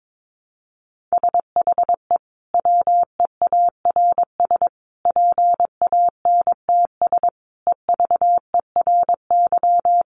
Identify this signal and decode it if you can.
Morse code, heard as a keyed tone, about 22 wpm, 700 Hz. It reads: SHE WEARS PANTS EVERY
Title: Additional Easy Phrases 22wpm